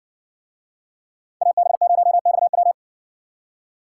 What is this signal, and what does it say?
Morse code, heard as a keyed tone, about 57 wpm, 700 Hz. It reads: AB0XW